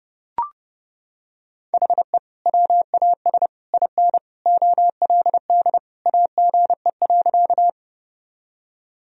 Morse code, heard as a keyed tone, about 32 wpm, 700 Hz. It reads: HE WAS IN OLD AGE.
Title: Sentences from Top 1000 Words - Review Part 3 - 30wpm